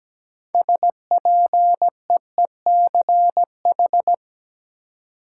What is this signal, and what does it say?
Morse code, heard as a keyed tone, about 17 wpm, 700 Hz. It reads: SPEECH
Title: Six Letter Words - 17wpm